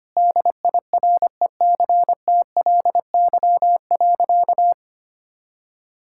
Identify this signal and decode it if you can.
Morse code, heard as a keyed tone, about 25 wpm, 700 Hz. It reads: DIRECTLY.
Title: Sentences from Top 1000 Words - Review Part 5 - 25wpm